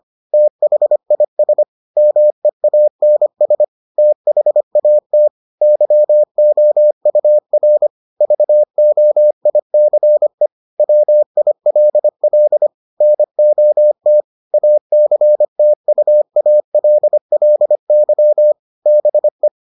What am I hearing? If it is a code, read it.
Morse code, heard as a keyed tone, about 25 wpm, 600 Hz. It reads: THIS MEANS THAT YOUR VOICE WILL NOT ACTUALLY BE